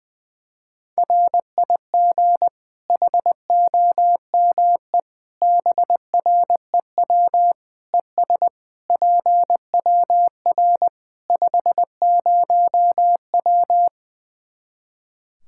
Morse code, heard as a keyed tone, about 20 wpm, 700 Hz. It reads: RIG HOME BREW ES PWR 50W
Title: Practice QSO - Exchange 2 - Info - 20wpm